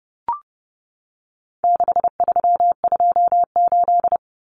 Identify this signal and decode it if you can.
Morse code, heard as a keyed tone, about 30 wpm, 700 Hz. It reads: 6328
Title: Common Cut Numbers - 4-Digit 30wpm